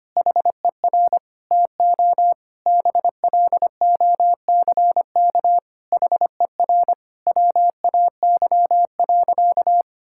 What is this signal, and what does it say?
Morse code, heard as a keyed tone, about 25 wpm, 700 Hz. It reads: HER TO BLOCK HER WAY.
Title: Sentences from Top 1600 Words 25wpm